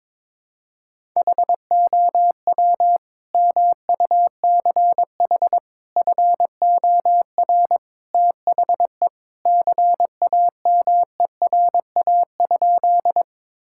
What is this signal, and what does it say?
Morse code, heard as a keyed tone, about 22 wpm, 700 Hz. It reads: HOW MUCH FOR THE CAMERA?
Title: Sentences from Top 1000 Words - Review Part 5 - 22wpm